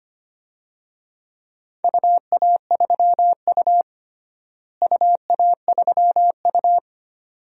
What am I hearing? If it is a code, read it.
Morse code, heard as a keyed tone, about 25 wpm, 700 Hz. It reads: UA3U UA3U
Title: DXCC-100 Call Signs - 4 Characters - Repeated 2x - 25wpm